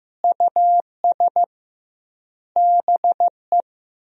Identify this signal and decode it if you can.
Morse code, heard as a keyed tone, about 15 wpm, 700 Hz. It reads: US BE
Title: 2x Word-Spacing - Sentences from Top 300 Words - 15wpm